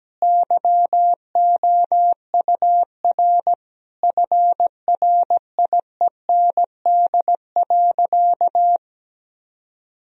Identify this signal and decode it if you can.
Morse code, heard as a keyed tone, about 17 wpm, 700 Hz. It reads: YOUR FRIEND.
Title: Sentences from Top 1000 Words - Review Part 4 - 17wpm